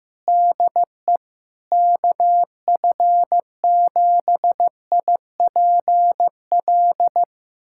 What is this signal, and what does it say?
Morse code, heard as a keyed tone, about 15 wpm, 700 Hz. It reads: DE KF7IPL
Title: USA Callsign DE Callsign - 6 Characters - 15wpm